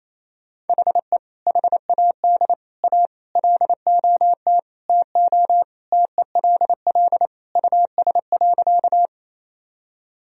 Morse code, heard as a keyed tone, about 28 wpm, 700 Hz. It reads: HE HAD A LOT TO TELL US.
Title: Sentences from Top 1000 Words - Review Part 3 - 28wpm